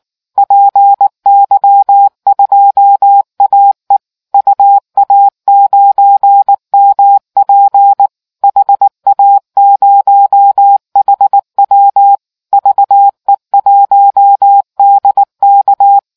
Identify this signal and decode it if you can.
Morse code, heard as a keyed tone, about 19 wpm, 800 Hz. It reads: PY2AE UA9MP HA0HW VE1DK